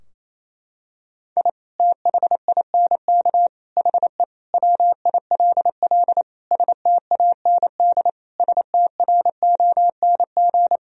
Morse code, heard as a keyed tone, about 28 wpm, 700 Hz. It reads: I THINK HE WILL STAND STRONG
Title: Sentences from Top 1000 Words - Review Part 3 - 28wpm